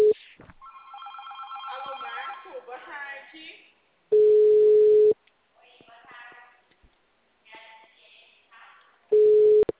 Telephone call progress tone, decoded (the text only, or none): ringback tone